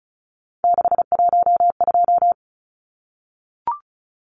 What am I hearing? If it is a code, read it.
Morse code, heard as a keyed tone, about 34 wpm, 700 Hz. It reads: 612
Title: Numbers - 3-Digit 35wpm